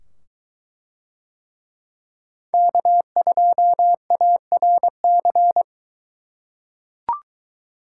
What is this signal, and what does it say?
Morse code, heard as a keyed tone, about 22 wpm, 700 Hz. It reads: K2ARC